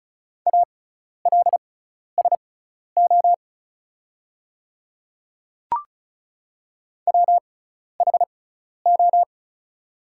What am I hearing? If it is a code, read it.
Morse code, heard as a keyed tone, about 36 wpm, spaced as for 12 wpm, 700 Hz. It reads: ALSO WHO
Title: 6x Character Spacing - Top 200 Words - 35wpm